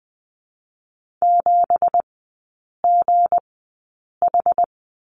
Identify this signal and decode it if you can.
Morse code, heard as a keyed tone, about 19 wpm, spaced as for 8 wpm, 700 Hz. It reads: 7GH
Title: 166 Lesson 27e - Number 7 - Three Characters 20wpm